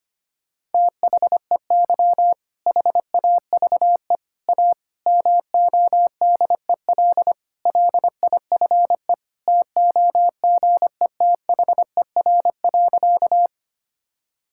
Morse code, heard as a keyed tone, about 25 wpm, 700 Hz. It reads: THEY HAVE A MODEL LIFE TOGETHER.
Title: Sentences from Top 1000 Words - Review Part 5 - 25wpm